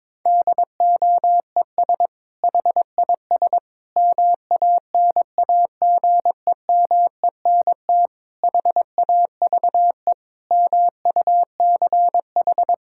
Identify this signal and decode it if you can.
Morse code, heard as a keyed tone, about 22 wpm, 700 Hz. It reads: DOES HIS MANAGEMENT HAVE MUCH
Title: Sentences from Top 1000 Words - Review Part 6 - 22wpm